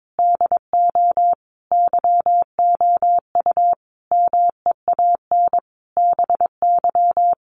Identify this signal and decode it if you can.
Morse code, heard as a keyed tone, about 22 wpm, 700 Hz. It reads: DO YOU MEAN BY
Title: Sentences from Top 200 Words 22wpm